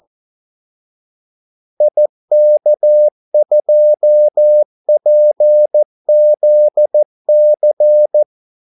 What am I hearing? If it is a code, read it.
Morse code, heard as a keyed tone, about 14 wpm, 600 Hz. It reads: IK2PZC
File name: Callsigns-14wpm